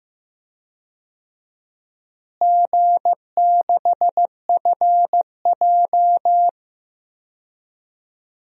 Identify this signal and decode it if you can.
Morse code, heard as a keyed tone, about 15 wpm, 700 Hz. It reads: G6FJ
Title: DXCC-100 Call Signs - 4 Characters - Repeated 3x - 15wpm